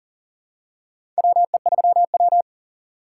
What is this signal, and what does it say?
Morse code, heard as a keyed tone, about 40 wpm, 700 Hz. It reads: WE3W
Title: USA Call Signs - 6x Character Spacing - 40wpm